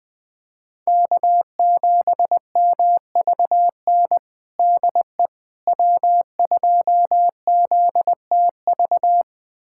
Morse code, heard as a keyed tone, about 20 wpm, 700 Hz. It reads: K7MVN DE W2ZTV
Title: USA Callsign DE Callsign 20wpm